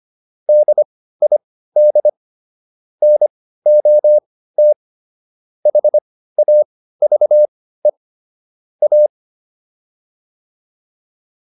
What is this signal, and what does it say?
Morse code, heard as a keyed tone, about 25 wpm, spaced as for 15 wpm, 600 Hz. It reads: DID NOT HAVE A